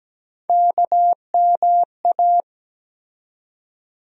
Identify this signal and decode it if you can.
Morse code, heard as a keyed tone, about 17 wpm, 700 Hz. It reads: KMA
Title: USA Call Sign Postfixes - 3 Letters 17wpm